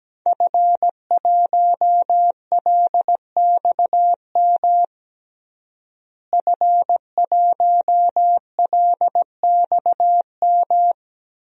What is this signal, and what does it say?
Morse code, heard as a keyed tone, about 17 wpm, 700 Hz. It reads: F1LXM F1LXM
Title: DXCC-100 Call Signs - 5 Characters - Repeated 2x - 17wpm